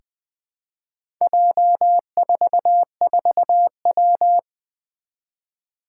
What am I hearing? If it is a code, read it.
Morse code, heard as a keyed tone, about 20 wpm, 700 Hz. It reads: J44W
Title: Four Challenging Letter-Number Combinations 20wpm